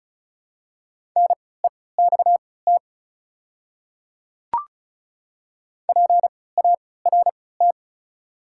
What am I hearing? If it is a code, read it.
Morse code, heard as a keyed tone, about 36 wpm, spaced as for 20 wpm, 690 Hz. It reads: NEXT PART
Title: 3x Character Spacing - Top 300 Words - 35wpm